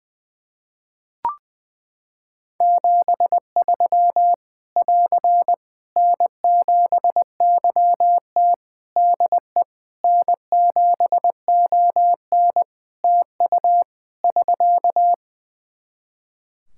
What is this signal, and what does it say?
Morse code, heard as a keyed tone, about 21 wpm, 700 Hz. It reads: 73 <AR> N7YT DE N7ON TU <SK>
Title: Practice QSO - Exchange 4 - Ending - 20wpm